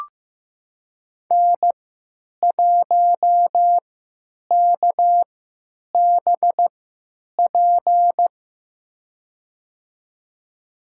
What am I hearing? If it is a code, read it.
Morse code, heard as a keyed tone, about 15 wpm, spaced as for 9 wpm, 700 Hz. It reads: N1KBP
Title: USA Call Signs - 3x Character Spacing - 15wpm